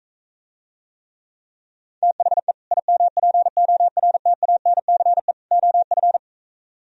Morse code, heard as a keyed tone, about 45 wpm, 700 Hz. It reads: THE IMPORTANCE OF